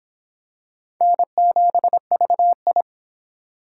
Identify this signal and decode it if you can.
Morse code, heard as a keyed tone, about 25 wpm, 700 Hz. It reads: N7VI